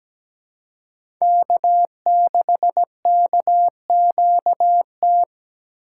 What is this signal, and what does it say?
Morse code, heard as a keyed tone, about 17 wpm, 700 Hz. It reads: K6KQT
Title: USA Call Signs - 5 Characters - 3x Character Spacing - 17wpm